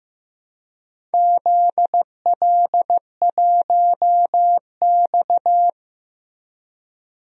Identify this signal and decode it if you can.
Morse code, heard as a keyed tone, about 15 wpm, 700 Hz. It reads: ZL1X
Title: DXCC-100 Call Signs - 4 Characters - Repeated 3x - 15wpm